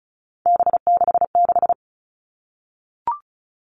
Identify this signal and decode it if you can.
Morse code, heard as a keyed tone, about 34 wpm, 700 Hz. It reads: B66
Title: All Cut Numbers - 3-Digit 35wpm